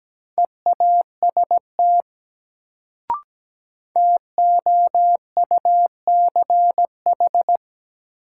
Morse code, heard as a keyed tone, about 17 wpm, 700 Hz. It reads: EAST TOUCH
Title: Sentences from Top 1200 Words - New Word Drill Alt - 17wpm